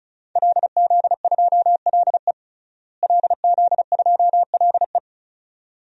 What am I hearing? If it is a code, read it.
Morse code, heard as a keyed tone, about 36 wpm, 700 Hz. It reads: LZ2LE LZ2LE
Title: DXCC-100 Call Signs - 5 Characters - Repeated 2x - 35wpm